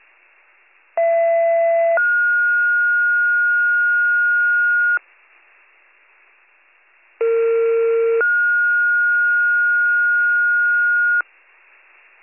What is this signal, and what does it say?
Two-tone page, heard as a one-second tone, then a three-second tone, about 670 then 1430 Hz, about 460 then 1430 Hz. It is heard from 1.0 to 11.2 s.